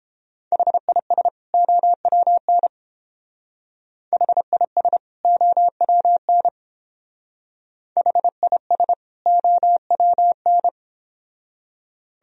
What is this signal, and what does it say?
Morse code, heard as a keyed tone, about 30 wpm, 700 Hz. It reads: HIS OWN HIS OWN HIS OWN